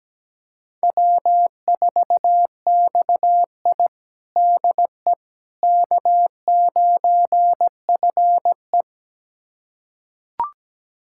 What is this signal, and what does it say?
Morse code, heard as a keyed tone, about 17 wpm, 700 Hz. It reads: W4XI DE K9FE
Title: USA Callsign DE Callsign - 4 Characters - 17wpm